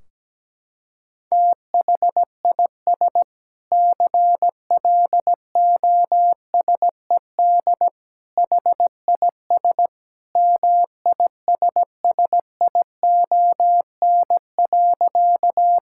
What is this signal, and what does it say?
Morse code, heard as a keyed tone, about 17 wpm, 700 Hz. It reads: THIS CLOSED HIS MISSION.
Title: Sentences from Top 1600 Words 17wpm